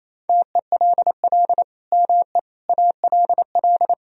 Morse code, heard as a keyed tone, about 28 wpm, 700 Hz. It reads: TELL ME ALL
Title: Sentences from Top 1000 Words - Review Part 4 - 28wpm